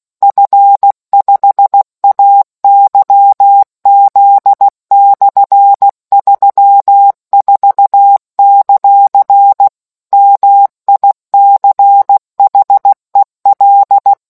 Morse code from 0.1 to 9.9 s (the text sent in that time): F5AYZ/34;